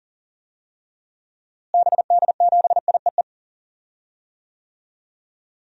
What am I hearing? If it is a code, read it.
Morse code, heard as a keyed tone, about 40 wpm, 700 Hz. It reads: BD7IEE
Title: DXCC-100 Call Signs - 6 Characters - Repeated 2x - 40wpm